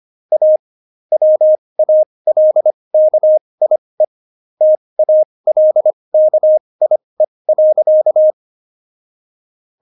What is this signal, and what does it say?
Morse code, heard as a keyed tone, about 25 wpm, spaced as for 20 wpm, 620 Hz. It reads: A WALKIE TALKIE.
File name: Jokes-25-char-20-eff